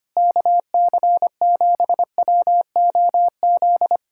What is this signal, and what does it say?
Morse code, heard as a keyed tone, about 24 wpm, 700 Hz. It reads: KC7WOZ
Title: Speed-Racing Common QSO Phrases Encore - 25wpm